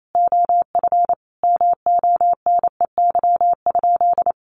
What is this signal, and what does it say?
Morse code, heard as a keyed tone, about 28 wpm, 700 Hz. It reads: OF MONEY?
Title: Sentences from Top 1100 Words 28wpm